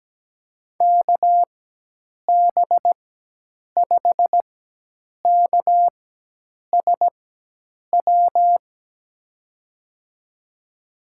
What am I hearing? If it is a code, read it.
Morse code, heard as a keyed tone, about 17 wpm, spaced as for 8 wpm, 700 Hz. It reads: KB5KSW
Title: USA Call Signs - 6 Characters - 4x Character Spacing - 17wpm